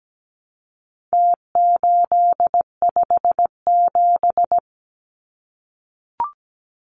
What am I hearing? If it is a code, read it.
Morse code, heard as a keyed tone, about 17 wpm, 700 Hz. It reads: T857
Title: Numbers - Contest Style - 4-Digit 17wpm